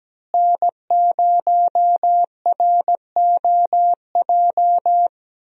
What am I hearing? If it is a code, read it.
Morse code, heard as a keyed tone, about 17 wpm, 700 Hz. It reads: N0ROJ
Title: USA Call Signs - 5 Characters - Repeated 3x - 17wpm